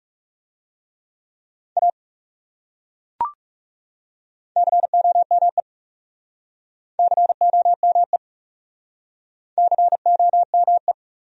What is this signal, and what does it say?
Morse code, heard as a keyed tone, about 40 wpm, 700 Hz. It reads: A COME COME COME